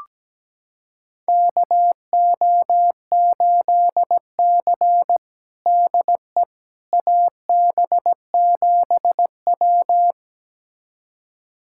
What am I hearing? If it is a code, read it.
Morse code, heard as a keyed tone, about 17 wpm, 700 Hz. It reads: KO8C DE AB7W
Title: USA Callsign DE Callsign - 4 Characters - 17wpm